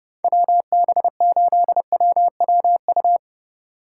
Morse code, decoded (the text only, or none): WB8WWU